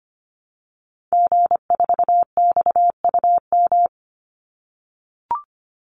Morse code, heard as a keyed tone, about 25 wpm, 700 Hz. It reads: G4XUM
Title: Call Signs - 25wpm